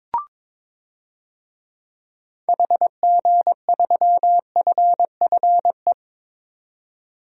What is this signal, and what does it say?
Morse code, heard as a keyed tone, about 22 wpm, 700 Hz. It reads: HG3FFE
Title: DXCC-100 Call Signs - 6 Characters - 22wpm